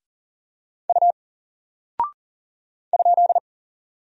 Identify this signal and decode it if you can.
Morse code, heard as a keyed tone, about 40 wpm, 700 Hz. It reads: U ?
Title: Single Character (Letter:Number:Punctuation:Prosign) Alt 40wpm